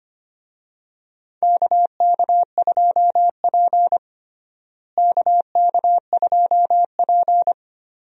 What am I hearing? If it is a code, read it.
Morse code, heard as a keyed tone, about 25 wpm, 700 Hz. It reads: KK2P KK2P
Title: USA Call Signs - 4 Characters - Repeated 3x - 25wpm